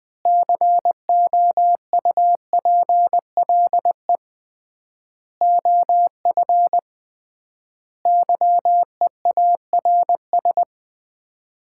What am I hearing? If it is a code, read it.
Morse code, heard as a keyed tone, about 20 wpm, 700 Hz. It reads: COUPLE OF YEARS
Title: 3x Word-Spacing - Sets of 5 Words - 20wpm